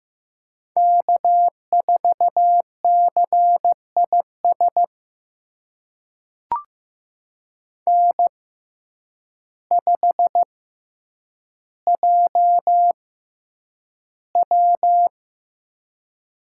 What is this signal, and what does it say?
Morse code, heard as a keyed tone, about 15 wpm, spaced as for 8 wpm, 700 Hz. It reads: K4CIS N5JW
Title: USA Call Signs - 5 Characters - 6x Character Spacing - 15wpm